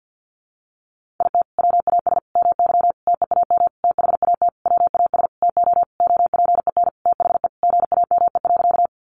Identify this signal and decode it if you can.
Morse code, heard as a keyed tone, about 50 wpm, 700 Hz. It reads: IT WAS MY TEAM THAT WAS TO OPEN THE GAME.
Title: Sentences from Top 1500 Words 50wpm